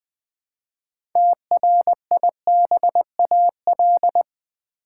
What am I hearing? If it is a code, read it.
Morse code, heard as a keyed tone, about 20 wpm, 700 Hz. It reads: TRIBAL
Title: Build A Word - 3 Letters At A Time - 6-Letter Words - 20wpm